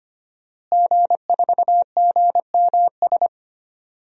Morse code, heard as a keyed tone, about 25 wpm, 700 Hz. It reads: G4GMS
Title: DXCC-100 Call Signs - 5 Characters - Repeated 3x - 25wpm